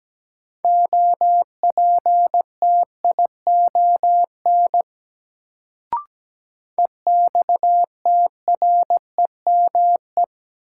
Morse code, heard as a keyed tone, about 17 wpm, 700 Hz. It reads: OPTION EXTREME
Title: Sentences from Top 1300 Words - New Word Drill Alt - 17wpm